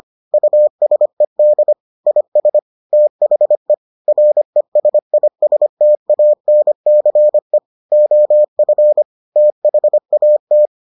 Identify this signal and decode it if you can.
Morse code, heard as a keyed tone, about 25 wpm, 600 Hz. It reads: USED IS THE RESISTANCE OF THAT